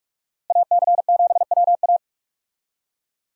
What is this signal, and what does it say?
Morse code, heard as a keyed tone, about 47 wpm, 700 Hz. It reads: AC7WA
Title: USA Call Signs - 5 Characters - Repeated 2x - 45wpm